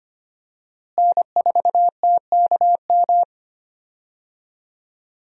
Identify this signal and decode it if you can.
Morse code, heard as a keyed tone, about 25 wpm, 700 Hz. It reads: N4TKM
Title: USA Call Signs - 5 Characters - Repeated 3x - 25wpm